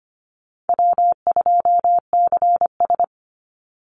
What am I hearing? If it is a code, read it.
Morse code, heard as a keyed tone, about 25 wpm, 700 Hz. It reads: W2CS
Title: Call Signs Repeated 3x - 25wpm